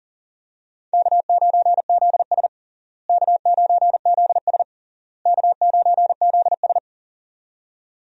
Morse code, heard as a keyed tone, about 40 wpm, 700 Hz. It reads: K9ZS K9ZS K9ZS
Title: USA Call Signs - 4 Characters - Repeated 3x - 40wpm